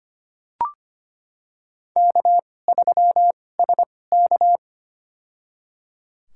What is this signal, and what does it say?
Morse code, heard as a keyed tone, about 25 wpm, spaced as for 18 wpm, 700 Hz. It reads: K3SK
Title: USA Call Signs - 4 Characters - 2x Character Spacing - 25wpm